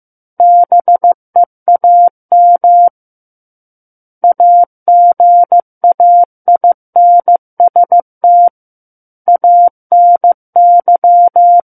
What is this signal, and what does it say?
Morse code, heard as a keyed tone, about 15 wpm, 700 Hz. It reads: BEAM AGAINST ANY
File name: Book One / 06-Chapter 6 - 15 WPM